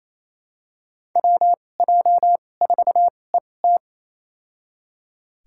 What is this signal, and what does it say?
Morse code, heard as a keyed tone, about 28 wpm, spaced as for 20 wpm, 700 Hz. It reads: WJ4ET
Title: USA Call Signs - 2x Character Spacing - 28wpm